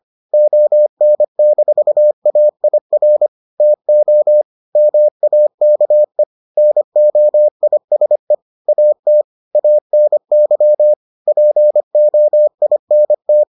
Morse code, heard as a keyed tone, about 25 wpm, 600 Hz. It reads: ON-AIR TO MAKE NOISE AT ANY POINT